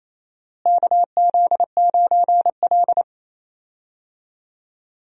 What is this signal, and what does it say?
Morse code, heard as a keyed tone, about 28 wpm, 700 Hz. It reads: KZ9L
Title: USA Call Signs - 4 Characters - Repeated 3x - 28wpm